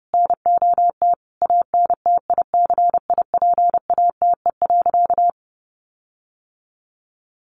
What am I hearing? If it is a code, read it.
Morse code, heard as a keyed tone, about 30 wpm, 700 Hz. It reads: NOT ANTICIPATE.
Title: Sentences from Top 1500 Words 30wpm